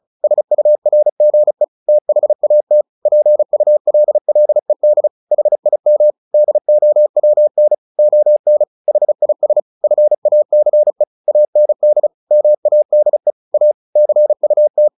Morse code, heard as a keyed tone, about 34 wpm, 600 Hz. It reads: SURGE THAT PULLED HIM DOWN ON HIS FACE AND MADE A CUT
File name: the old man and the sea / the old man and the sea 1 35wpm